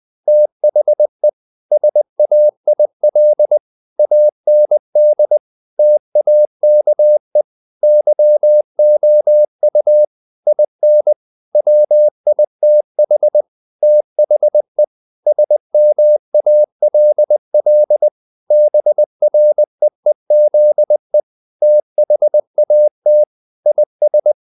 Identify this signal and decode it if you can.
Morse code, heard as a keyed tone, about 20 wpm, 600 Hz. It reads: THE SAIL AND TAKE YOU IN WITH THE SMALL BREEZE THAT IS
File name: the old man and the sea / the old man and the sea part 1 20wpm